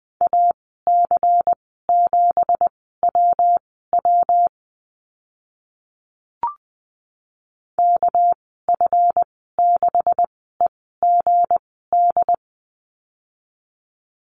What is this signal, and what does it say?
Morse code, heard as a keyed tone, about 19 wpm, spaced as for 14 wpm, 700 Hz. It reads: AC7WW KF6EGD
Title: 219 Lesson 35g - Number 0 - Call Signs 20wpm